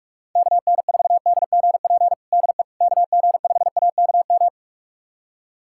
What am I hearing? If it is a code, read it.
Morse code, heard as a keyed tone, about 46 wpm, 700 Hz. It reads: KN4DGP DE KG5AKM
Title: USA Callsign DE Callsign - 6 Characters - 45wpm